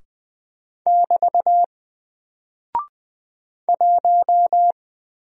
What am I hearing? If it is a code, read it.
Morse code, heard as a keyed tone, about 20 wpm, 700 Hz. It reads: <BT> 1
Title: Single Character (Letter:Number:Punctuation:Prosign) 20wpm